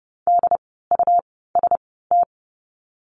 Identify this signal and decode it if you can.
Morse code, heard as a keyed tone, about 30 wpm, spaced as for 17 wpm, 700 Hz. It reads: DUST
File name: Lesson 17f - Letter u - Words 30wpm